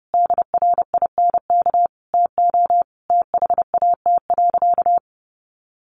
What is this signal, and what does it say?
Morse code, heard as a keyed tone, about 30 wpm, 700 Hz. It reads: DRINK TO THAT.
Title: Sentences from Top 1500 Words 30wpm